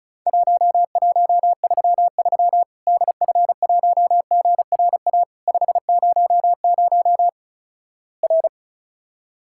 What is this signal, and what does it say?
Morse code, heard as a keyed tone, about 35 wpm, 700 Hz. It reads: 1133 DF1GRA 500 R
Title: Worked All Europe DX Contest - QTC Exchange - 35wpm